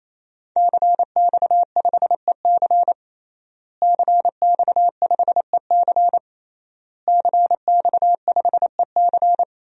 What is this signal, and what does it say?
Morse code, heard as a keyed tone, about 28 wpm, 700 Hz. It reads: CX5EC CX5EC CX5EC
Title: DXCC-100 Call Signs - 5 Characters - Repeated 3x - 28wpm